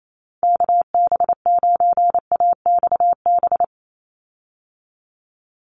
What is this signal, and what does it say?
Morse code, heard as a keyed tone, about 28 wpm, 700 Hz. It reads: KB9AXB
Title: USA Call Signs 28wpm